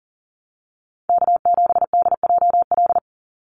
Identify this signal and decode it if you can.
Morse code, heard as a keyed tone, about 40 wpm, 700 Hz. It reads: K7DJL
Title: NA SOTA Chaser Callsigns 40wpm